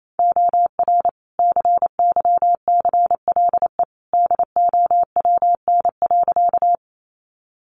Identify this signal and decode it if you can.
Morse code, heard as a keyed tone, about 28 wpm, 700 Hz. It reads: OR CYCLE DOWN.
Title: Sentences from Top 900 Words 28wpm